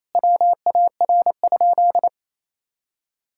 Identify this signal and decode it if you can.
Morse code, heard as a keyed tone, about 28 wpm, 700 Hz. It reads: WAR?
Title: Sentences from Top 1000 Words - Review Part 5 - 28wpm